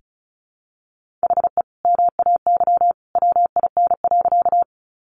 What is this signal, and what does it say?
Morse code, heard as a keyed tone, about 34 wpm, 700 Hz. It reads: HE MAY WIN.
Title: Sentences from Top 1500 Words 35wpm